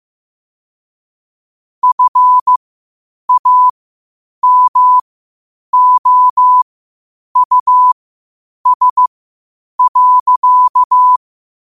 Morse code, heard as a keyed tone, about 15 wpm, spaced as for 8 wpm, 1000 Hz. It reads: FAMOUS.
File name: qotdfiles / 8wpm